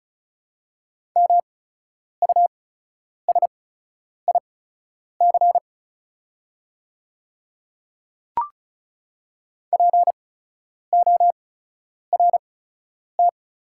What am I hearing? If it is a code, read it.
Morse code, heard as a keyed tone, about 36 wpm, spaced as for 10 wpm, 690 Hz. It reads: MUSIC PORT
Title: 8x Character Spacing - Top 500 Words - 35wpm